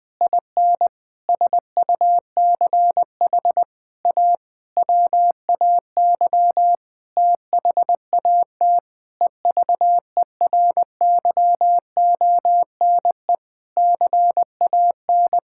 Morse code, heard as a keyed tone, about 21 wpm, 700 Hz. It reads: IN SUCH A WAY THAT EVERYONE CAN
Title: Sentences with 5-Word Phrases - Spoken Beginning - 20wpm